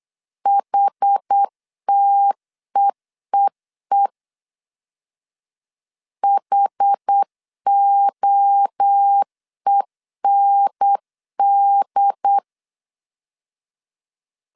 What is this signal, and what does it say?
Morse code, heard as a keyed tone, about 8 wpm, 800 Hz. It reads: HTEEE HOEND